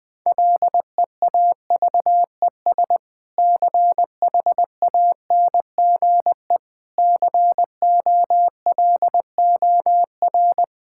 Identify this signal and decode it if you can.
Morse code, heard as a keyed tone, about 20 wpm, 700 Hz. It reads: LEAVES CHANGE COLOR
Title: Difficult Sentences with Hints - 20wpm